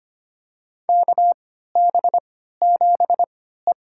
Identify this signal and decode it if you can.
Morse code, heard as a keyed tone, about 25 wpm, spaced as for 14 wpm, 700 Hz. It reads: KB7E
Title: USA Call Signs - 4 Characters - 3x Character Spacing - 25wpm